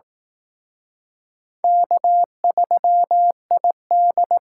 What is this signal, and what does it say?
Morse code, heard as a keyed tone, about 18 wpm, 700 Hz. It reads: K3ID
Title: Speed Racing - USA 1x2 Call Signs 15wpm